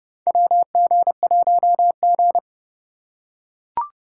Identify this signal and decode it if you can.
Morse code, heard as a keyed tone, about 30 wpm, 700 Hz. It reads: WG1G
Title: USA Call Signs - 4 Characters - 4x Character Spacing - 30wpm